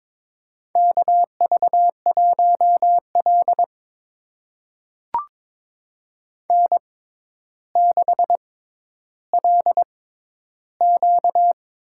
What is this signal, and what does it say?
Morse code, heard as a keyed tone, about 22 wpm, spaced as for 11 wpm, 700 Hz. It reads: KV1L N6LQ
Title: USA Call Signs - 4 Characters - 6x Character Spacing - 22wpm